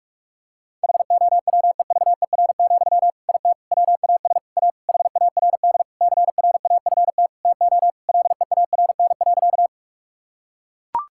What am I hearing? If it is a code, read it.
Morse code, heard as a keyed tone, about 41 wpm, 700 Hz. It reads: HOWEVER, IT WAS A HARD CRAFT TO LEARN.
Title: Sentences from Top 1000 Words - Review Part 6 - 45wpm